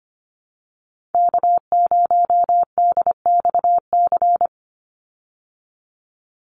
Morse code, heard as a keyed tone, about 25 wpm, 700 Hz. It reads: K0DXC
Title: Call Signs Repeated 2x - 25wpm